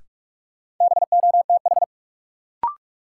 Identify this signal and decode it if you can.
Morse code, heard as a keyed tone, about 46 wpm, 700 Hz. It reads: BOTH
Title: Sets of 4 Words - New Word Drill - 45wpm